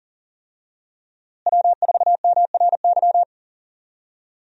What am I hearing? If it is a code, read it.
Morse code, heard as a keyed tone, about 40 wpm, 700 Hz. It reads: W4MRY
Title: USA Call Signs - 5 Characters - Repeated 2x - 40wpm